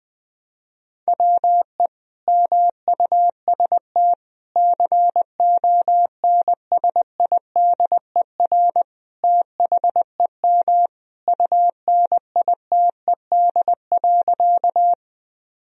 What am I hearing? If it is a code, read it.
Morse code, heard as a keyed tone, about 20 wpm, 700 Hz. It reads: WE MUST CONSIDER THEM UNITED.
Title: Sentences from Top 1000 Words - Review Part 4 - 20wpm